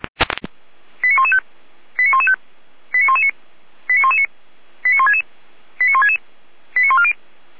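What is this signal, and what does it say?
Five-tone selective call (ZVEI), heard as quick runs of five tones, tones of about 70 ms. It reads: 89115 89115 89119 89119 89166 89166 89140
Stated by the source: ZVEI five-tone call received by SDR.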